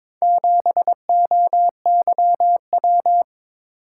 Morse code, heard as a keyed tone, about 22 wpm, 700 Hz. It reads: 7OYW
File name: practice / 22wpm / Four Letter-Number Combinations - 22wpm